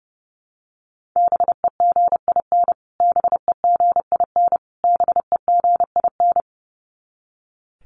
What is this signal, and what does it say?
Morse code, heard as a keyed tone, about 30 wpm, 700 Hz. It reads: BEGIN BEGIN BEGIN
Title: Top 500 Words Repeated 30wpm